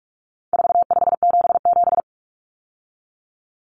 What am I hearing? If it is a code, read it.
Morse code, heard as a keyed tone, about 44 wpm, 700 Hz. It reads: 4577
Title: Common Cut Numbers - 4-Digit 45wpm